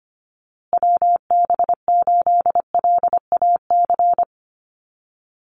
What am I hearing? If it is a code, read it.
Morse code, heard as a keyed tone, about 25 wpm, 700 Hz. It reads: WB8LAC
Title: USA Call Signs Repeated 2x 25wpm